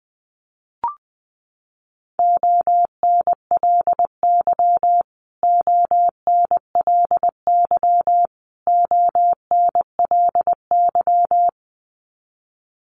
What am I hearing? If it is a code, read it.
Morse code, heard as a keyed tone, about 19 wpm, 700 Hz. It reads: ONLY ONLY ONLY
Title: Top 300 Words Repeated 20wpm